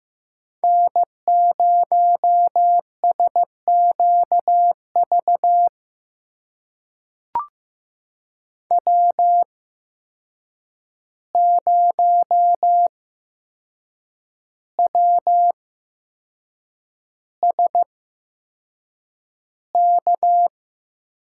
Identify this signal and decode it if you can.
Morse code, heard as a keyed tone, about 15 wpm, spaced as for 6 wpm, 700 Hz. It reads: N0SQV W0WSK
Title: USA Call Signs - 5 Characters - 8x Character Spacing - 15wpm